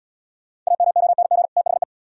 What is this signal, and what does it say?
Morse code, heard as a keyed tone, about 67 wpm, 700 Hz. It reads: AMONG THE